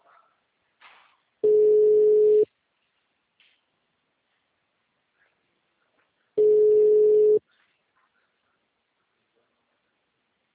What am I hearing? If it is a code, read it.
Ringback tone (Europe, 425 Hz).